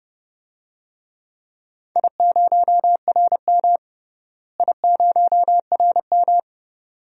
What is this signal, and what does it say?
Morse code, heard as a keyed tone, about 30 wpm, 700 Hz. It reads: I0RM I0RM
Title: DXCC-100 Call Signs - 4 Characters - Repeated 2x - 30wpm